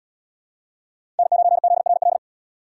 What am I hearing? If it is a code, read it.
Morse code, heard as a keyed tone, about 71 wpm, 700 Hz. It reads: N9ZRL